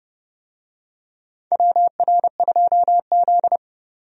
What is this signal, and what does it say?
Morse code, heard as a keyed tone, about 30 wpm, 700 Hz. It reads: WR2Z
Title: USA Call Signs - 4 Characters - 30wpm